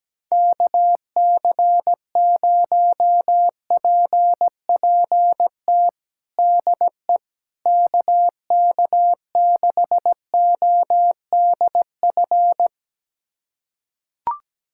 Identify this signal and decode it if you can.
Morse code, heard as a keyed tone, about 17 wpm, 700 Hz. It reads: KC0PPT DE KK6ODF
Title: USA Callsign DE Callsign 17wpm